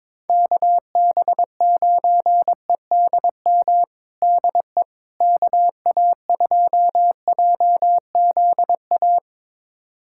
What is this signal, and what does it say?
Morse code, heard as a keyed tone, about 22 wpm, 700 Hz. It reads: KB9EDM DE KA2JZA
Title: USA Callsign DE Callsign 22wpm